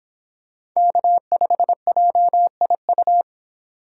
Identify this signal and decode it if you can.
Morse code, heard as a keyed tone, about 26 wpm, 700 Hz. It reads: K5JIU